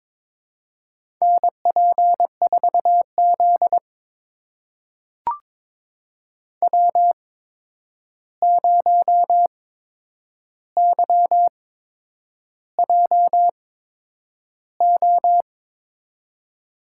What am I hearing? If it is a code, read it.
Morse code, heard as a keyed tone, about 22 wpm, spaced as for 9 wpm, 700 Hz. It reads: NP4Z W0YJO
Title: USA Call Signs - 8x Character Spacing - 22wpm